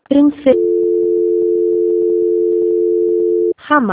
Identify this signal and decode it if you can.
Dial tone (North America), 350 + 440 Hz, with speech.
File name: out- - - - - .20921